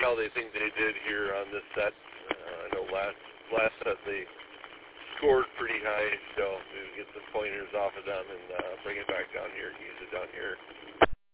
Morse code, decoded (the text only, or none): none